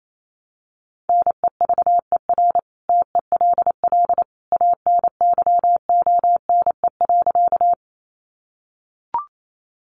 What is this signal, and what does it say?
Morse code, heard as a keyed tone, about 28 wpm, 700 Hz. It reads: NEVER TELL ANYONE.
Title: Sentences from Top 900 Words 28wpm